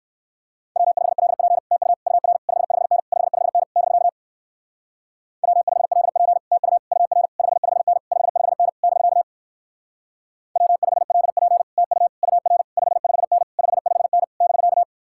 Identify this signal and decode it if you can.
Morse code, heard as a keyed tone, about 57 wpm, 700 Hz. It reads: W5LP TU UR 55N 55N <BK> W5LP TU UR 55N 55N <BK> W5LP TU UR 55N 55N <BK>